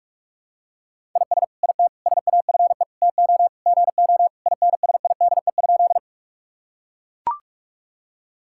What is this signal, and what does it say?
Morse code, heard as a keyed tone, about 42 wpm, 700 Hz. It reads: IS IT SAFE TO GO INSIDE?
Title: Sentences from Top 1000 Words - Review Part 6 - 45wpm